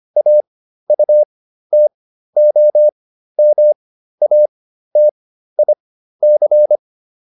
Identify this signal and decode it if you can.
Morse code, heard as a keyed tone, about 25 wpm, spaced as for 13 wpm, 600 Hz. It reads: AUTOMATIC